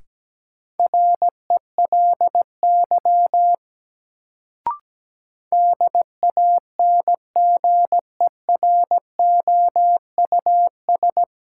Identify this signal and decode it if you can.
Morse code, heard as a keyed tone, about 17 wpm, 700 Hz. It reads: RELY DANGEROUS
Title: Sentences from Top 1200 Words - New Word Drill - 17wpm